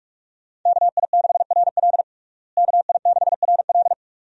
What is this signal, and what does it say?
Morse code, heard as a keyed tone, about 42 wpm, 700 Hz. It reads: KI6RL KI6RL
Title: USA Call Signs - 5 Characters - Repeated 2x - 45wpm